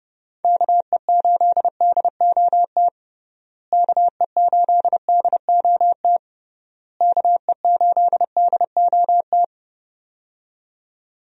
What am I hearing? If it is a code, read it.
Morse code, heard as a keyed tone, about 31 wpm, 700 Hz. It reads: KE8DOT KE8DOT KE8DOT
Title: USA Call Signs - 6 Characters - Repeated 3x - 30wpm